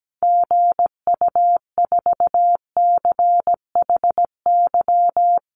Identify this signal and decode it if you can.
Morse code, heard as a keyed tone, about 17 wpm, 700 Hz. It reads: GU4CHY
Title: Call Signs - 17wpm